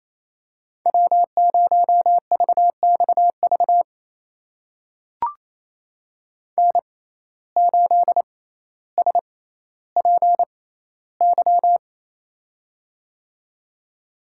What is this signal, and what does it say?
Morse code, heard as a keyed tone, about 28 wpm, spaced as for 14 wpm, 700 Hz. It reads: W0VXV N8SPY
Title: USA Call Signs - 5 Characters - 6x Character Spacing - 28wpm